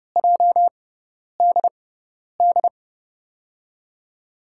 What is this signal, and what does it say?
Morse code, heard as a keyed tone, about 30 wpm, spaced as for 10 wpm, 700 Hz. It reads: JDD
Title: USA Call Signs - 6 Characters - 6x Character Spacing - 30wpm